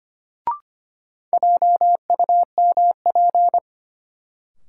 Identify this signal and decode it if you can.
Morse code, heard as a keyed tone, about 25 wpm, 700 Hz. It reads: JUMP
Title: Sentences from Top 1300 Words - New Word Drill Alt - 25wpm